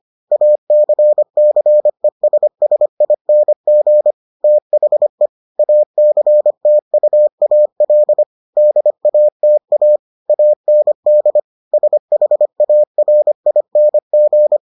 Morse code, heard as a keyed tone, about 25 wpm, 600 Hz. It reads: ACCESSING THE ACTUAL DATA AND SHARING